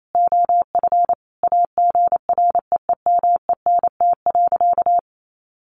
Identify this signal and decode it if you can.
Morse code, heard as a keyed tone, about 28 wpm, 700 Hz. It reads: OF AGREEMENT.
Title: Sentences from Top 1500 Words 28wpm